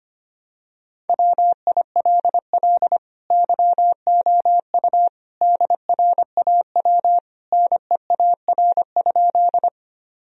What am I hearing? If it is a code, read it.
Morse code, heard as a keyed tone, about 25 wpm, 700 Hz. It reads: WILL YOU DRAW NEAR?
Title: Sentences from Top 1000 Words - Review Part 1 - 25wpm